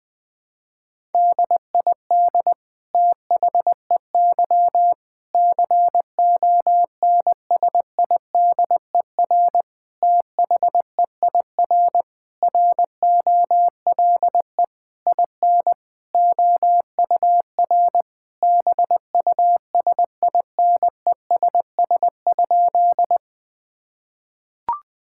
Morse code, heard as a keyed tone, about 20 wpm, 700 Hz. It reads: DID THEY CONSIDER THEIR ROLE IN OUR BUSINESS?
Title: Sentences from Top 1000 Words - Review Part 6 - 20wpm